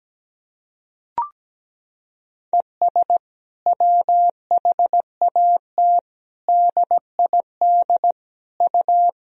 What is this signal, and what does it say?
Morse code, heard as a keyed tone, about 17 wpm, 700 Hz. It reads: ES WHAT DID U
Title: Sentences from Top 100 Words and Abbreviations 17wpm